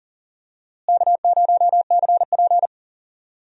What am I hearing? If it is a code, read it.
Morse code, heard as a keyed tone, about 40 wpm, 700 Hz. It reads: K0CP
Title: USA Call Signs - 4 Characters - Repeated 2x - 40wpm